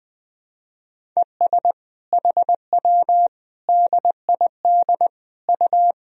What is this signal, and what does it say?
Morse code, heard as a keyed tone, about 21 wpm, 700 Hz. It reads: ES HW DID U
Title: Sentences from Top 100 Words and Abbreviations 20wpm